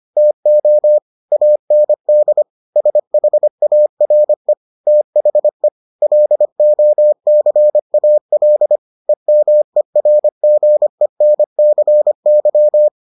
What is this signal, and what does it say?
Morse code, heard as a keyed tone, about 25 wpm, 600 Hz. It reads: TO AND SHARE THE LOCAL EMERGENCY